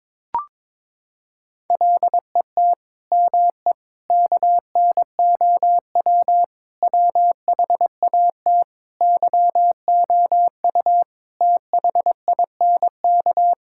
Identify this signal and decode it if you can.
Morse code, heard as a keyed tone, about 22 wpm, 700 Hz. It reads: LET ME KNOW WHAT YOU THINK
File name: Sets of 9 Words - 22wpm